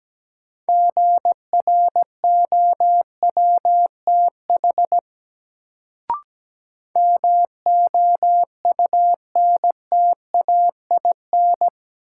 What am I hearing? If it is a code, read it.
Morse code, heard as a keyed tone, about 17 wpm, 700 Hz. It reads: GROWTH MOUNTAIN
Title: Sentences from Top 1300 Words - New Word Drill Alt - 17wpm